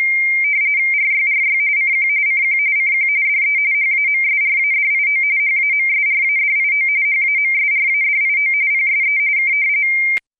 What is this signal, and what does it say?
RTTY, about 45.45 baud. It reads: RYRY CQ CQ CQ DE HAMSRADIO HAMSRADIO HAMSRADIO PSE K